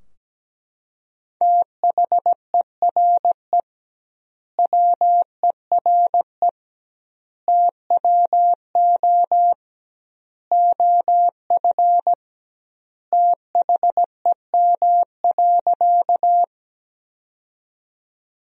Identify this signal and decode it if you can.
Morse code, heard as a keyed tone, about 17 wpm, 700 Hz. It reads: THERE WERE TWO OF THEM.
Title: 2x Word-Spacing - Sentences from Top 100 Words - 17wpm